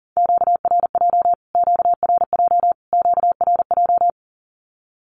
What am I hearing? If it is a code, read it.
Morse code, heard as a keyed tone, about 40 wpm, 700 Hz. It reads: QRJ QRJ QRJ
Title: Q-Codes Repeated 3x 40wpm